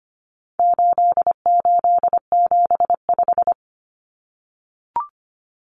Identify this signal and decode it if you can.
Morse code, heard as a keyed tone, about 25 wpm, 700 Hz. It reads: 8875
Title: Common Cut Numbers - 4-Digit 25wpm